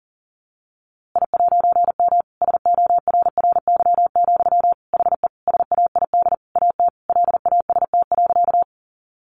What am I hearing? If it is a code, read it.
Morse code, heard as a keyed tone, about 40 wpm, 700 Hz. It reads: I'M SORRY, HE SAID AT LAST.
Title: A Sound Of Thunder 40wpm